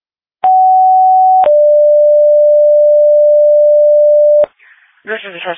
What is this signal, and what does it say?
Two-tone page, heard as a one-second tone, then a three-second tone, about 750 then 590 Hz, with speech. It is heard from 0.4 to 4.4 s.